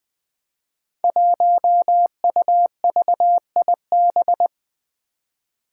Morse code, heard as a keyed tone, about 20 wpm, 700 Hz. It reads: 1UVIB
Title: Five Challenging Letter-Number Combinations 20wpm